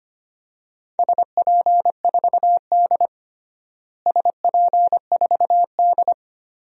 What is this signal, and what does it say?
Morse code, heard as a keyed tone, about 25 wpm, 700 Hz. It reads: SP4D SP4D
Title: DXCC-100 Call Signs - 4 Characters - Repeated 3x - 25wpm